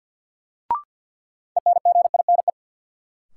Morse code, heard as a keyed tone, about 51 wpm, 700 Hz. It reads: ENGINE
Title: Sentences from Top 1500 Words - New Word Drill Alt - 50wpm